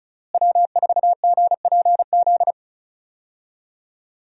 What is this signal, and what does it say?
Morse code, heard as a keyed tone, about 34 wpm, 700 Hz. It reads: W4GPZ
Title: USA Call Signs - 5 Characters - Repeated 3x - 35wpm